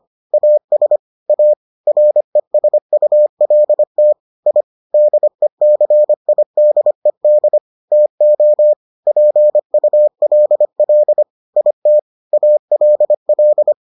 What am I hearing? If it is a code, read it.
Morse code, heard as a keyed tone, about 25 wpm, 600 Hz. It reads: AS A RESULT I DECIDED TO PULL IT ALL